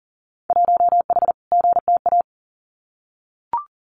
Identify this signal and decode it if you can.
Morse code, heard as a keyed tone, about 40 wpm, 700 Hz. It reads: 1H GTA
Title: Winter Field Day Exchanges 40wpm